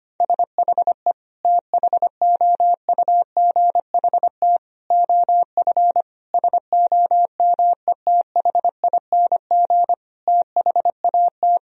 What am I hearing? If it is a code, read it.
Morse code, heard as a keyed tone, about 25 wpm, 700 Hz. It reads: SHE THOUGHT OF SOMETHING THAT
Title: Sentences from Top 1000 Words - Review Part 1 - 25wpm